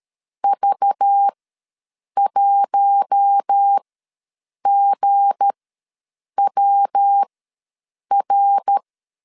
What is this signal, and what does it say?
Morse code, heard as a keyed tone, about 13 wpm, spaced as for 7 wpm, 800 Hz. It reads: V1GWR
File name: LEZIONI / 7 GF9Z / 7 GF ws 12wpm 3140